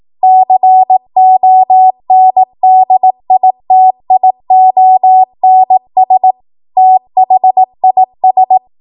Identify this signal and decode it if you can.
Morse code, heard as a keyed tone, about 18 wpm, 750 Hz. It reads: CONDITIONS THIS
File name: Morse / 18WPM